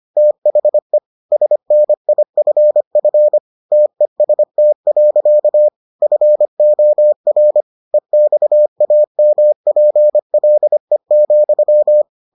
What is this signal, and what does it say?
Morse code, heard as a keyed tone, about 25 wpm, 600 Hz. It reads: THE SNIFF TEST. FOR EXAMPLE,